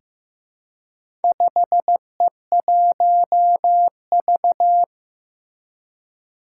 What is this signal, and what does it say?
Morse code, heard as a keyed tone, about 15 wpm, 700 Hz. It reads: 5E1V
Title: Four Challenging Letter-Number Combinations 15wpm